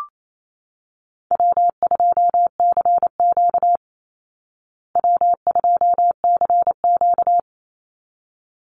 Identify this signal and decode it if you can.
Morse code, heard as a keyed tone, about 28 wpm, 700 Hz. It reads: W2CQ W2CQ
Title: Easy Call Signs Repeated 2x - 28wpm